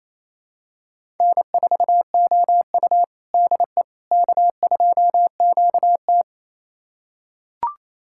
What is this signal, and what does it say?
Morse code, heard as a keyed tone, about 28 wpm, 700 Hz. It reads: N4OU DE K2QT
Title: USA Callsign DE Callsign - 4 Characters - 28wpm